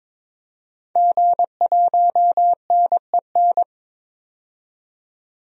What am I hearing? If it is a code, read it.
Morse code, heard as a keyed tone, about 22 wpm, 700 Hz. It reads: G1NEN
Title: DXCC-100 Call Signs - 5 Characters - 22wpm